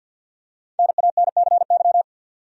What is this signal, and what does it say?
Morse code, heard as a keyed tone, about 52 wpm, 700 Hz. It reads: NANCY
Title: USA Common Names - 50wpm